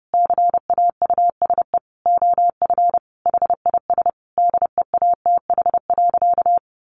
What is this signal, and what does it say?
Morse code, heard as a keyed tone, about 30 wpm, 700 Hz. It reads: CAUSE OF HIS DEATH.
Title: Sentences from Top 1100 Words 30wpm